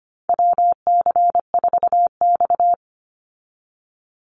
Morse code, heard as a keyed tone, about 24 wpm, 700 Hz. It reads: WC4X
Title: Call Signs - 25wpm